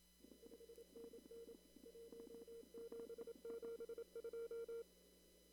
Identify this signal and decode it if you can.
Morse code, heard as a keyed tone, about 27 wpm, 460 Hz. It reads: 5NN 1772
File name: CW / kt5j wpx cw-op-k5pi